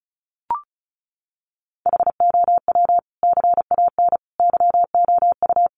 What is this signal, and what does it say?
Morse code, heard as a keyed tone, about 34 wpm, 700 Hz. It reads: HOW CAN YOU
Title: Sentences from Top 100 Words 35wpm